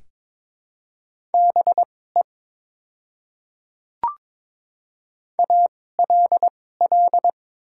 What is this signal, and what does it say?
Morse code, heard as a keyed tone, about 22 wpm, spaced as for 16 wpm, 700 Hz. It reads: BE ALL
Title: 2x Character Spacing - Top 100 Words - 22wpm